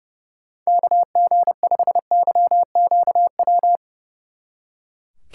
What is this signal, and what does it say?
Morse code, heard as a keyed tone, about 36 wpm, 700 Hz. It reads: KG5YQW
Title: Speed Racing - USA 2x3 Call Signs 30wpm